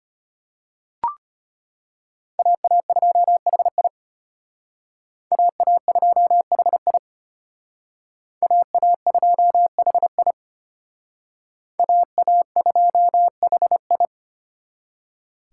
Morse code, heard as a keyed tone, about 33 wpm, 700 Hz. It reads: AA2HI AA2HI AA2HI AA2HI